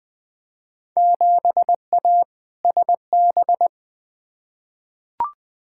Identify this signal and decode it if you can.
Morse code, heard as a keyed tone, about 20 wpm, 700 Hz. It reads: 7A SB
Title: Field Day Exchanges Encore - 20wpm